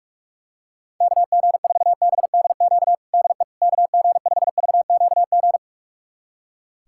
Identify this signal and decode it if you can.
Morse code, heard as a keyed tone, about 46 wpm, 700 Hz. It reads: KG4BDQ DE KG5VQG
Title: USA Callsign DE Callsign 45wpm